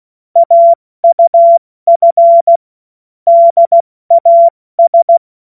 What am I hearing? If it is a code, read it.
Morse code, heard as a keyed tone, about 16 wpm, 680 Hz. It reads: AUF DAS
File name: Witz-15-Juli-2025-70-F80